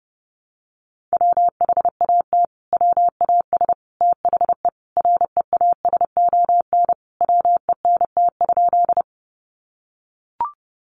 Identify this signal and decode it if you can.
Morse code, heard as a keyed tone, about 30 wpm, 700 Hz. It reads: WHAT WAS THE REASON WENT?
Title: Sentences from Top 700 Words 30wpm